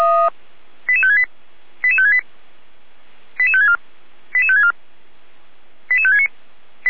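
Five-tone selective call (ZVEI), heard as quick runs of five tones, tones of about 70 ms. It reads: 88568 88568 88564 88564 88569
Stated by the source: SDR-received ZVEI five-tone selective call.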